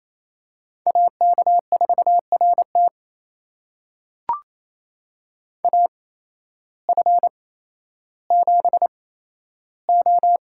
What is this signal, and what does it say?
Morse code, heard as a keyed tone, about 28 wpm, spaced as for 13 wpm, 700 Hz. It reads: AK4RT AF7O
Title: USA Call Signs - 8x Character Spacing - 28wpm